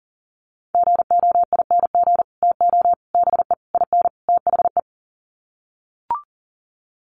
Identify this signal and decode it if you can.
Morse code, heard as a keyed tone, about 37 wpm, 700 Hz. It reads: GOING TO BE IN THE
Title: Sets of 5 Words Encore 40wpm